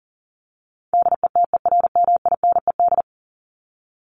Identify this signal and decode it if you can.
Morse code, heard as a keyed tone, about 40 wpm, 700 Hz. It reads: DETERMINED
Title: ED Suffix - Top 100 Words with Base Words 40wpm